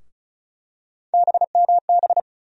Morse code, heard as a keyed tone, about 35 wpm, 700 Hz. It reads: BMB
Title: USA Call Sign Postfixes - 3 Letters 35wpm